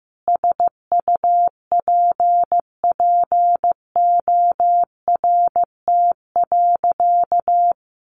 Morse code, heard as a keyed tone, about 15 wpm, 700 Hz. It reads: SUPPORT.
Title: Sentences from Top 1300 Words 15wpm